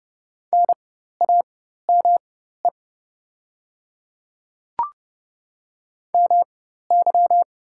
Morse code, heard as a keyed tone, about 30 wpm, spaced as for 14 wpm, 700 Hz. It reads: NAME MY
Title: 4x Character Spacing - Top 200 Words - 30wpm